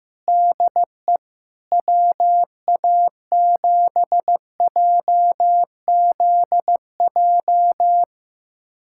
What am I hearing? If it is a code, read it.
Morse code, heard as a keyed tone, about 15 wpm, 700 Hz. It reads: DE WA7JZJ
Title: USA Callsign DE Callsign - 6 Characters - 15wpm